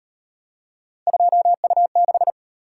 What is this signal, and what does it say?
Morse code, heard as a keyed tone, about 38 wpm, 700 Hz. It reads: 2U6